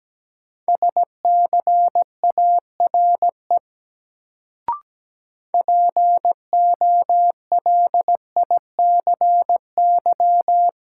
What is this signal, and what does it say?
Morse code, heard as a keyed tone, about 17 wpm, 700 Hz. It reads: SCARE POLICY
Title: Sentences from Top 1000 Words - Review Part 5 - New Word Drill - 17wpm